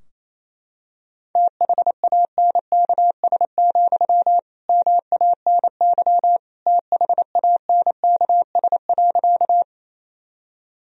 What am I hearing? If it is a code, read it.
Morse code, heard as a keyed tone, about 28 wpm, 700 Hz. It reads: THANKS, MANY THANKS.
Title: Sentences from Top 1000 Words - Review Part 3 - 28wpm